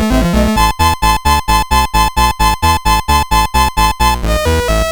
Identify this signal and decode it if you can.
There is no code.